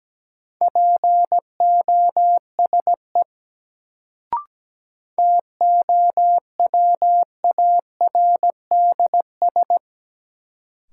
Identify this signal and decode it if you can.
Morse code, heard as a keyed tone, about 17 wpm, 700 Hz. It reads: POSE TOWARDS
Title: Sentences from Top 1000 Words - Review Part 2 - New Word Drill - 17wpm